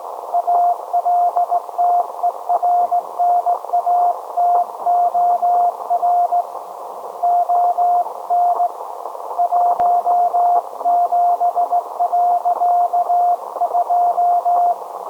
Morse code, heard as a keyed tone, about 17 wpm, 710 Hz. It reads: ALTERNATOR ON 17.2
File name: storage / message Grimeton SAQ